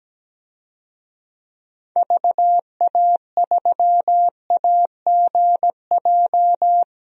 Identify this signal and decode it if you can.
Morse code, heard as a keyed tone, about 17 wpm, 700 Hz. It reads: VA3AGJ
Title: DXCC-100 Call Signs - 6 Characters - Repeated 3x - 17wpm